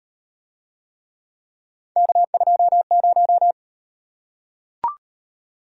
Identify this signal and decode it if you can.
Morse code, heard as a keyed tone, about 38 wpm, 700 Hz. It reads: K20